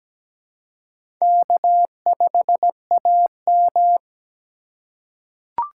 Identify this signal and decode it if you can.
Morse code, heard as a keyed tone, about 17 wpm, 700 Hz. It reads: K5AM
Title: K1USN SST QSO Practice - 17wpm